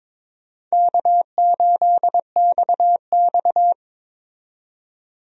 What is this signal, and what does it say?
Morse code, heard as a keyed tone, about 22 wpm, 700 Hz. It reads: K8XX
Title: USA Call Signs - 4 Characters - 22wpm